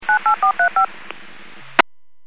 DTMF, heard as telephone touch tones, tones of about 110 ms. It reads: 98435
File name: WPS / 02050 14-05 WPS 451.25 WPS